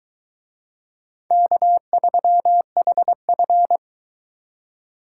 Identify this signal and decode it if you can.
Morse code, heard as a keyed tone, about 22 wpm, 700 Hz. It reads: K3HF